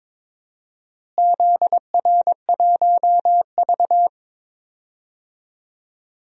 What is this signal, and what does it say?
Morse code, heard as a keyed tone, about 22 wpm, 700 Hz. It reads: ZR1V
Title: DXCC-100 Call Signs - 4 Characters - Repeated 3x - 22wpm